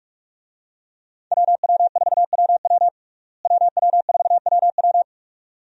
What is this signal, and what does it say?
Morse code, heard as a keyed tone, about 46 wpm, 700 Hz. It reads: WW4WW WW4WW
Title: USA Call Signs - 5 Characters - Repeated 2x - 45wpm